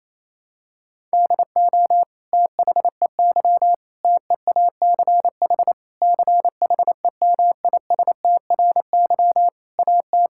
Morse code, heard as a keyed tone, about 28 wpm, 700 Hz. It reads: DO THEY TEACH CHEMISTRY AT
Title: Sentences from Top 1000 Words - Review Part 6 - 28wpm